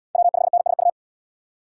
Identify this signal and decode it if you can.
Morse code, heard as a keyed tone, about 73 wpm, 700 Hz. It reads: W5NSU